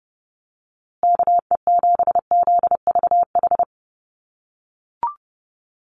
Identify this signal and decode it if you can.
Morse code, heard as a keyed tone, about 30 wpm, 700 Hz. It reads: KE7ZVH
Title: USA Call Signs Repeated 3x 30wpm